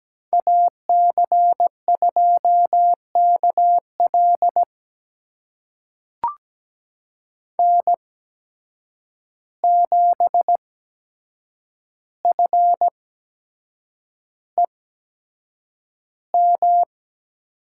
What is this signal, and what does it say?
Morse code, heard as a keyed tone, about 17 wpm, spaced as for 7 wpm, 700 Hz. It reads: AC2KL N7FEM
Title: USA Call Signs - 5 Characters - 8x Character Spacing - 17wpm